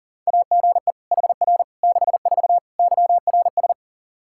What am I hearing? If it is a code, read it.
Morse code, heard as a keyed tone, about 40 wpm, 700 Hz. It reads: AGE HR 64 YRS
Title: Practice QSO - Exchange 3 - More Info - 40wpm